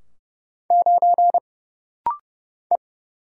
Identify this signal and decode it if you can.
Morse code, heard as a keyed tone, about 30 wpm, 700 Hz. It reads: 9 E
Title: Single Letter-Number - Rapid-Fire 30wpm